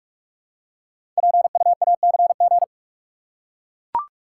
Morse code, heard as a keyed tone, about 44 wpm, 700 Hz. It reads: PUACG
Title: Code Groups - 5 Letters - 45wpm